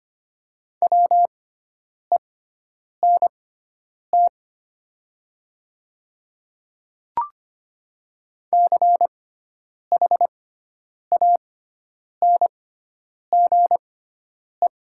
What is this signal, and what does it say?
Morse code, heard as a keyed tone, about 25 wpm, spaced as for 9 wpm, 700 Hz. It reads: WENT CHANGE
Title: 6x Character Spacing - Top 300 Words - 25wpm